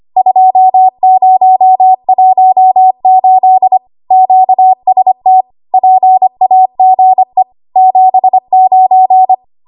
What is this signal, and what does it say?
Morse code, heard as a keyed tone, about 25 wpm, 750 Hz. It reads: 2018 QST PAGE 79
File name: Archive / 25 WPM / 25WPM